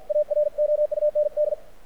Morse code, heard as a keyed tone, about 54 wpm, 600 Hz. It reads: AROUND
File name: archived / CRSnet